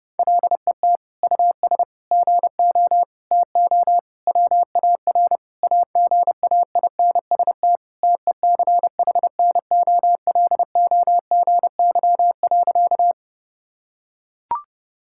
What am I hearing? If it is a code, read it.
Morse code, heard as a keyed tone, about 32 wpm, 700 Hz. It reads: LET US GO TO WAR AGAINST TECHNOLOGY.
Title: Sentences from Top 1000 Words - Review Part 5 - 30wpm